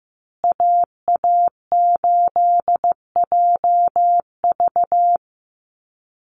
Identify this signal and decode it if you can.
Morse code, heard as a keyed tone, about 15 wpm, 700 Hz. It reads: AA8JV
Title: USA Call Signs Repeated 2x 15wpm